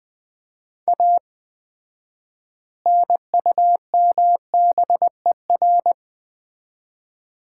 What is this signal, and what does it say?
Morse code, heard as a keyed tone, about 20 wpm, 700 Hz. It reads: A NUMBER
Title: 4x Word-Spacing - Sets of 3 Words - 20wpm